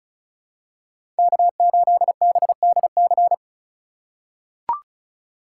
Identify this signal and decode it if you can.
Morse code, heard as a keyed tone, about 36 wpm, 700 Hz. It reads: K8BDC
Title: USA Call Signs - 5 Characters - 4x Character Spacing - 35wpm